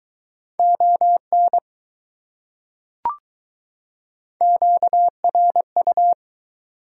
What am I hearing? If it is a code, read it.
Morse code, heard as a keyed tone, about 21 wpm, 700 Hz. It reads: ON QRU